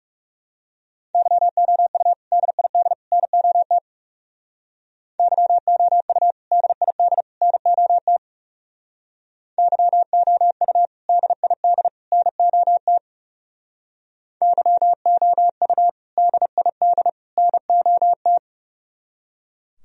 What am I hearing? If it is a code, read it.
Morse code, heard as a keyed tone, about 37 wpm, 700 Hz. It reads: YOU DID NOT YOU DID NOT YOU DID NOT YOU DID NOT
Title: Speed-Racing - Sets of 3 Words 30wpm